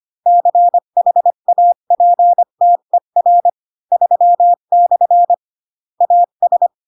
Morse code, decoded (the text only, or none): CHAPTER 3/ AS